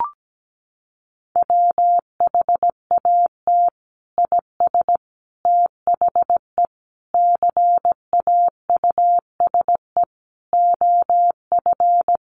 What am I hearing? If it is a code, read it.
Morse code, heard as a keyed tone, about 17 wpm, 700 Hz. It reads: WHAT IS THE CAUSE OF
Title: Sentences from Top 1100 Words 17wpm